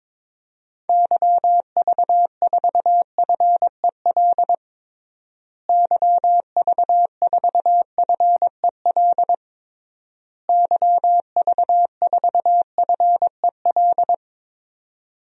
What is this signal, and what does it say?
Morse code, heard as a keyed tone, about 22 wpm, 700 Hz. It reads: YV4FEL YV4FEL YV4FEL
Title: DXCC-100 Call Signs - 6 Characters - Repeated 3x - 22wpm